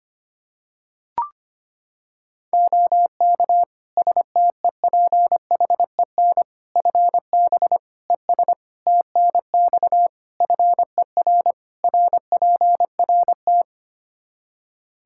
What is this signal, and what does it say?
Morse code, heard as a keyed tone, about 25 wpm, 700 Hz. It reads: OK STEPHEN FB ES TNX FER RPRT
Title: Practice QSO - Exchange 2 - Info - 25wpm